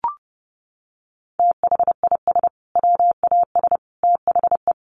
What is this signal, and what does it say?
Morse code, heard as a keyed tone, about 30 wpm, 700 Hz. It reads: THIS WAS THE
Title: Sentences from Top 900 Words 30wpm